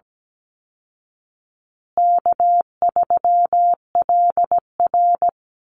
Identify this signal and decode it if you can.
Morse code, heard as a keyed tone, about 17 wpm, 700 Hz. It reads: K3LR
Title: Easy Call Signs Repeated 3x - 17wpm